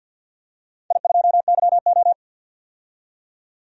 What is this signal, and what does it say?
Morse code, heard as a keyed tone, about 47 wpm, 700 Hz. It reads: I2YO
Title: DXCC-100 Call Signs - 4 Characters - 50wpm